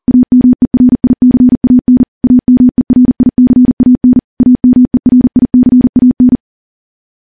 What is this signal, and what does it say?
Morse code, heard as a keyed tone, about 40 wpm, 250 Hz. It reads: AMERICAN AMERICAN AMERICAN
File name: morse / morse-training-40WPM